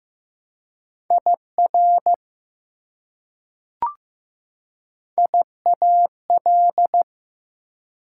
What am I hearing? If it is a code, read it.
Morse code, heard as a keyed tone, about 16 wpm, 700 Hz. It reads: IR IAL
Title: Prefixes and Suffixes 15wpm